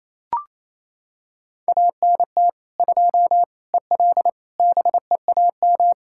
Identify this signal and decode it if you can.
Morse code, heard as a keyed tone, about 27 wpm, 700 Hz. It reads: ANT 2 EL BEAM
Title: Practice QSO - Exchange 2 - Info - 28wpm